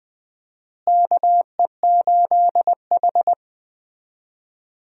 Morse code, heard as a keyed tone, about 20 wpm, 700 Hz. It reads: KE8H
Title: USA Call Signs - 4 Characters - Repeated 3x - 20wpm